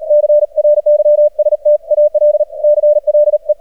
Morse code, heard as a keyed tone, about 40 wpm, 600 Hz. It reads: CRYSTAL CLE
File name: archived / CRSnet